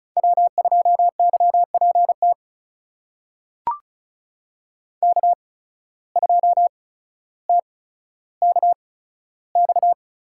Morse code, heard as a keyed tone, about 36 wpm, spaced as for 15 wpm, 700 Hz. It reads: W2YPT K2TKX
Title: USA Call Signs - 5 Characters - 8x Character Spacing - 35wpm